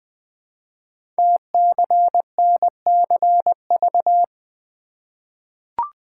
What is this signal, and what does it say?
Morse code, heard as a keyed tone, about 20 wpm, 700 Hz. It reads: TCNCV
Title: Code Groups - 5 Letters - 20wpm